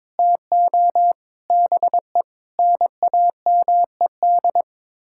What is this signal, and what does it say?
Morse code, heard as a keyed tone, about 22 wpm, 700 Hz. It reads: TO BE NAMED
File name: practice / 22wpm / Sets of 7 Words - 22wpm